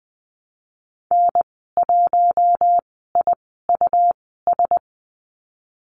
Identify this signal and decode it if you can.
Morse code, heard as a keyed tone, about 20 wpm, spaced as for 15 wpm, 700 Hz. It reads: N1IUS
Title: 173 Lesson 28g - Number 9 - Call Signs 20wpm